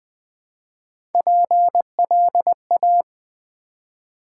Morse code, heard as a keyed tone, about 20 wpm, 700 Hz. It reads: PLA
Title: Build A Word - 1 Letter At A Time - 4-Letter Words - 20wpm